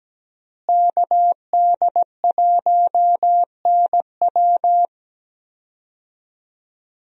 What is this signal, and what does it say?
Morse code, heard as a keyed tone, about 17 wpm, 700 Hz. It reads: KD1NW
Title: USA Call Signs - 5 Characters - Repeated 3x - 17wpm